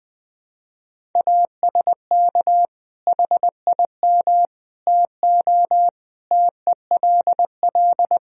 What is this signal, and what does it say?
Morse code, heard as a keyed tone, about 20 wpm, 700 Hz. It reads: ASK HIM TO TELL
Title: Sentences from Top 1000 Words - Review Part 3 - 20wpm